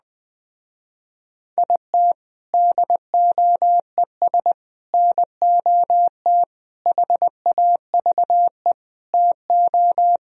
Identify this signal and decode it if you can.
Morse code, heard as a keyed tone, about 20 wpm, 700 Hz. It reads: IT DOES NOT HAVE TO